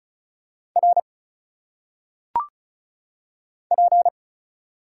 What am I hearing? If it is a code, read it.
Morse code, heard as a keyed tone, about 36 wpm, 700 Hz. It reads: R P
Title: Single Letters - Phonetic Pronunciation 35wpm